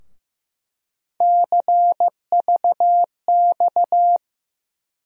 Morse code, heard as a keyed tone, about 16 wpm, 700 Hz. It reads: CVX
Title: USA Call Sign Postfixes - 3 Letters 15wpm